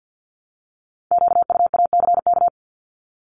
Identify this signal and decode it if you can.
Morse code, heard as a keyed tone, about 50 wpm, 690 Hz. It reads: QUACK
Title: Multi-letter Phonemes Repeated 3x 50wpm